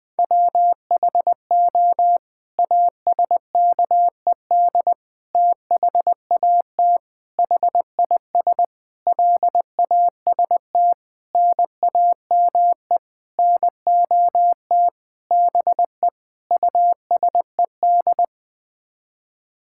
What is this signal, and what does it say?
Morse code, heard as a keyed tone, about 21 wpm, 700 Hz. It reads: WHO ASKED THAT HIS LAST NAME NOT BE USED
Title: Sets of 9 Words - 20wpm